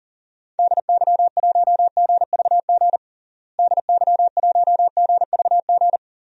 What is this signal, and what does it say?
Morse code, heard as a keyed tone, about 40 wpm, 700 Hz. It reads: DY1GVG DY1GVG
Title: DXCC-100 Call Signs - 6 Characters - Repeated 2x - 40wpm